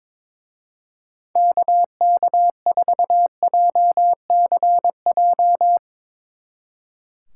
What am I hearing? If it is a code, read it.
Morse code, heard as a keyed tone, about 24 wpm, 700 Hz. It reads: KK4JCJ
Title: Speed Racing - USA 2x3 Call Signs 22wpm